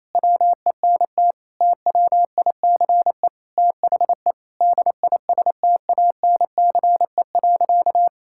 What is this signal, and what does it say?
Morse code, heard as a keyed tone, about 28 wpm, 700 Hz. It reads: WENT TWICE THE DISTANCE.
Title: Sentences from Top 1600 Words 28wpm